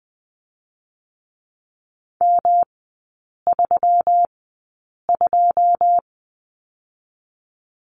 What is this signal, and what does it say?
Morse code, heard as a keyed tone, about 20 wpm, spaced as for 8 wpm, 700 Hz. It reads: M32
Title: 145 Lesson 24e - Letter y - Three Characters 20wpm